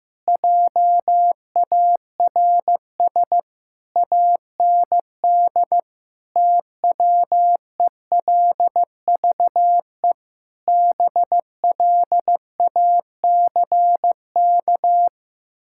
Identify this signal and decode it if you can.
Morse code, heard as a keyed tone, about 15 wpm, 700 Hz. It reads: JARS AND TWELVE BLACK
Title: Pangram Challenge v2 - 15wpm